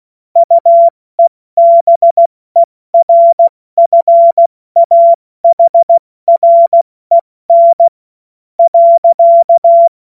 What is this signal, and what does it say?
Morse code, heard as a keyed tone, about 16 wpm, spaced as for 13 wpm, 680 Hz. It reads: UEBERFAHREN.
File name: Morsekurs / Witze / Witz 12. Oktober 2021, 70 F80